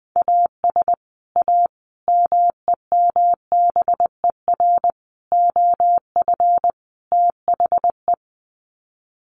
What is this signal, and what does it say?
Morse code, heard as a keyed tone, about 20 wpm, 700 Hz. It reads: AS A MEMBER OF THE
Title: Sets of 5 Words 20wpm